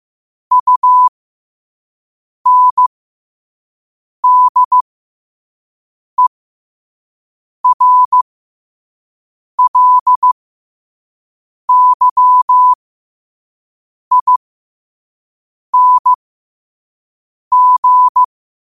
Morse code, heard as a keyed tone, about 15 wpm, spaced as for 5 wpm, 1000 Hz. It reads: UNDERLYING
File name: qotdfiles / 5wpm